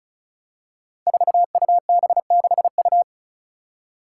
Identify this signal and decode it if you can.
Morse code, heard as a keyed tone, about 34 wpm, 700 Hz. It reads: 4UB6U
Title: Five Challenging Letter-Number Combinations 35wpm